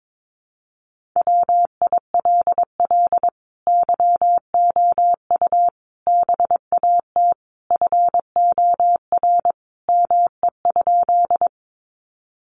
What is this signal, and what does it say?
Morse code, heard as a keyed tone, about 22 wpm, 700 Hz. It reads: WILL YOU BAT FOR ME?
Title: Sentences from Top 1500 Words 22wpm